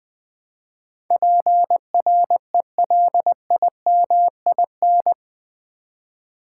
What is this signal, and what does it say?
Morse code, heard as a keyed tone, about 20 wpm, 700 Hz. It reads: PRELIMIN
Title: Build A Word - 4 Letters At A Time - 11-Letter Words - 20wpm